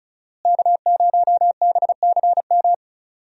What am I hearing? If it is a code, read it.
Morse code, heard as a keyed tone, about 36 wpm, 700 Hz. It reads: K0BCM
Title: USA Call Signs - 5 Characters - Repeated 2x - 35wpm